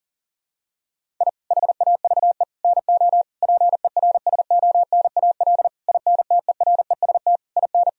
Morse code, heard as a keyed tone, about 40 wpm, 700 Hz. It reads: I HAVE NO PERSONAL INTEREST IN
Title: Sentences from Top 1000 Words - Review Part 4 - 40wpm